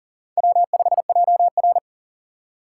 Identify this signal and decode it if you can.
Morse code, heard as a keyed tone, about 40 wpm, 700 Hz. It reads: W5JR
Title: USA Call Signs - 4 Characters - 4x Character Spacing - 40wpm